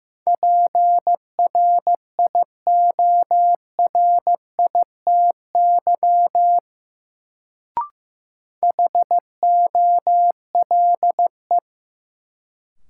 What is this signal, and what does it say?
Morse code, heard as a keyed tone, about 15 wpm, 700 Hz. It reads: PRIORITY HOLE
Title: Sentences from Top 1600 Words - New Word Drill - 15wpm